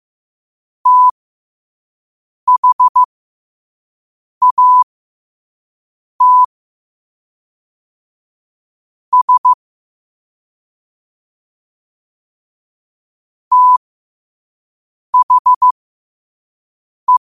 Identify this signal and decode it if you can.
Morse code, heard as a keyed tone, about 15 wpm, spaced as for 5 wpm, 1000 Hz. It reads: THAT S THE
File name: qotdfiles / 5wpm